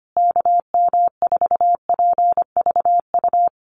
Morse code, heard as a keyed tone, about 25 wpm, 700 Hz. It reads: KM4PVU
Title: USA Call Signs Repeated 3x 25wpm